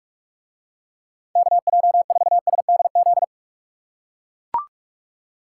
Morse code, heard as a keyed tone, about 44 wpm, 700 Hz. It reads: KJ4SDZ